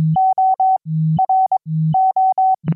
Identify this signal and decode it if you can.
Morse code, heard as a keyed tone, about 22 wpm, spaced as for 14 wpm, 760 Hz. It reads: ORO
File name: QSO-PA3FXB-DJ5AR